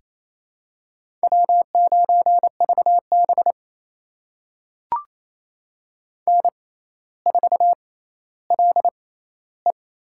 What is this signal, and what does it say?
Morse code, heard as a keyed tone, about 28 wpm, spaced as for 14 wpm, 700 Hz. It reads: W9VB N4LE
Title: USA Call Signs - 4 Characters - 6x Character Spacing - 28wpm